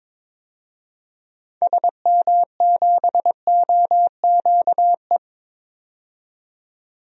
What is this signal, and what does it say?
Morse code, heard as a keyed tone, about 22 wpm, 700 Hz. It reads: SM7OQE
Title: DXCC-100 Call Signs - 6 Characters - 22wpm